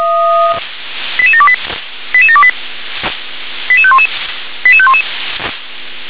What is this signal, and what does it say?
Five-tone selective call (ZVEI), heard as quick runs of five tones, tones of about 70 ms. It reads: 88728 88728 88511 88511
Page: /357/SDR/ZVEI/